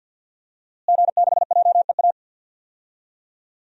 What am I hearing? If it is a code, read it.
Morse code, heard as a keyed tone, about 53 wpm, 700 Hz. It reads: G6JEA
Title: DXCC-100 Call Signs - 5 Characters - Repeated 3x - 50wpm